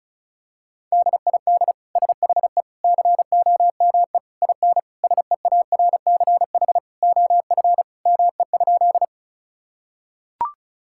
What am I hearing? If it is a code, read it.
Morse code, heard as a keyed tone, about 36 wpm, 700 Hz. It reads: DID SHE COME IN SEARCH OF ME?
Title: Sentences from Top 1000 Words - Review Part 5 - 35wpm